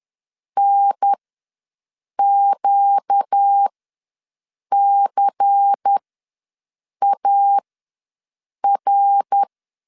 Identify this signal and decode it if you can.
Morse code, heard as a keyed tone, about 10 wpm, spaced as for 6 wpm, 800 Hz. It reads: NQCAR
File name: LEZIONI / 5 LBVC / 5 LBVC ws 10wpm 3733